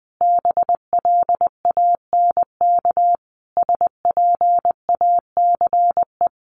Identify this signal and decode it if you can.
Morse code, heard as a keyed tone, about 20 wpm, 700 Hz. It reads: BLANK SPACE
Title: Sentences from Top 1500 Words 20wpm